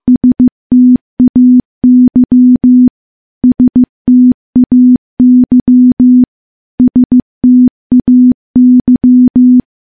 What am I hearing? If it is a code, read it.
Morse code, heard as a keyed tone, about 15 wpm, 250 Hz. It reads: STAY STAY STAY
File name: morse-training-15WPM